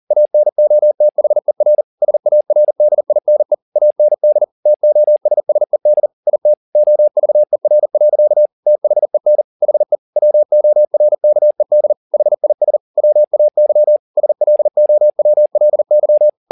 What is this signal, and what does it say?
Morse code, heard as a keyed tone, about 40 wpm, 600 Hz. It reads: ANOTHER SARDINE AND TOSSED IT OVER. THEN HE WORKED HIS WAY SLOWLY
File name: the old man and the sea 2 40wpm